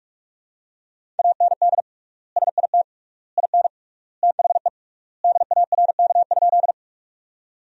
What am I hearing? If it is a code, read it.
Morse code, heard as a keyed tone, about 46 wpm, 700 Hz. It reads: AND SIT IN THE DARK?
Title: 3x Word-Spacing - Sentences from Top 500 Words - 45wpm